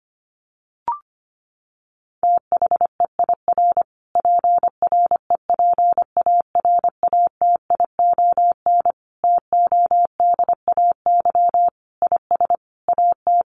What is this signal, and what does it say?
Morse code, heard as a keyed tone, about 25 wpm, 700 Hz. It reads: THEIR PREPARATION TODAY IS AT
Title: Sentences from Top 1400 Words 25wpm